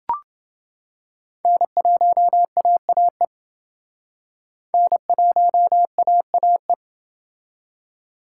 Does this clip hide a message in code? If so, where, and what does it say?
Yes: Morse code from 1.4 to 6.7 s, heard as a keyed tone, about 29 wpm, 700 Hz. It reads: N1AAE N1AAE